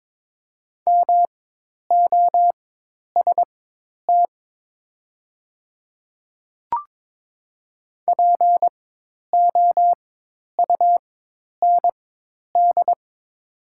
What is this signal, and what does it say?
Morse code, heard as a keyed tone, about 22 wpm, spaced as for 10 wpm, 700 Hz. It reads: MOST POUND
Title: 4x Character Spacing - Top 500 Words - 22wpm